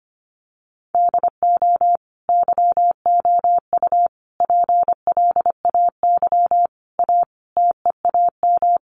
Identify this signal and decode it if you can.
Morse code, heard as a keyed tone, about 25 wpm, 700 Hz. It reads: DO YOU PLAY A TEAM
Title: Sentences from Top 1500 Words 25wpm